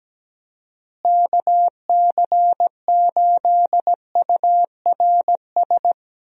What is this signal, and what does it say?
Morse code, heard as a keyed tone, about 17 wpm, 700 Hz. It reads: KC8URS
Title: USA Call Signs - 6 Characters - Repeated 3x - 17wpm